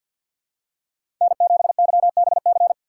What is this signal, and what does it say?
Morse code, heard as a keyed tone, about 47 wpm, 700 Hz. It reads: N7YBC
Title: USA Call Signs - 5 Characters - 4x Character Spacing - 50wpm